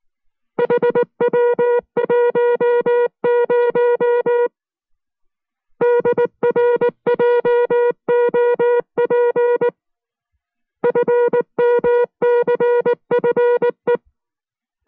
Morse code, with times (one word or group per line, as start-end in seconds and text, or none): 0.6-4.5 HW10
5.8-9.7 DRJOP
10.8-14.0 FMCFE